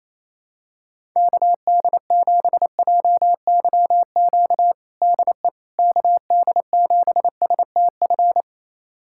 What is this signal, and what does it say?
Morse code, heard as a keyed tone, about 28 wpm, 700 Hz. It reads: KD7JYQ DE KD7STF
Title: USA Callsign DE Callsign 28wpm